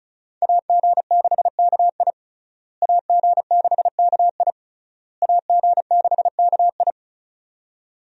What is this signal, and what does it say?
Morse code, heard as a keyed tone, about 34 wpm, 700 Hz. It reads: AG6KI AG6KI AG6KI
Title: USA Call Signs - 5 Characters - Repeated 3x - 35wpm